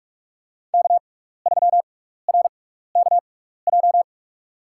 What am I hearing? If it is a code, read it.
Morse code, heard as a keyed tone, about 47 wpm, spaced as for 16 wpm, 700 Hz. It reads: K3RKJ
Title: USA Call Signs - 5 Characters - 6x Character Spacing - 45wpm